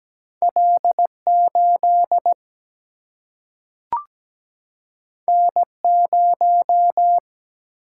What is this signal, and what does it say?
Morse code, heard as a keyed tone, about 17 wpm, 700 Hz. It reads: L8 N0
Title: Single-Letter Single-Number 17wpm